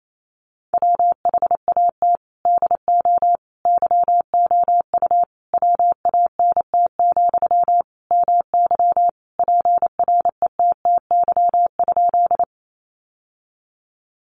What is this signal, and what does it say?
Morse code, heard as a keyed tone, about 28 wpm, 700 Hz. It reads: WHAT DO YOU WANT, MY PRETTY?
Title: Sentences from Top 1000 Words 28wpm